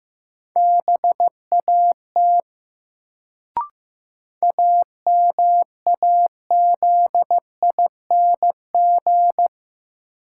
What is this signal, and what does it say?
Morse code, heard as a keyed tone, about 15 wpm, 700 Hz. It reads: BAT AMAZING
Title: Sentences from Top 1500 Words - New Word Drill - 15wpm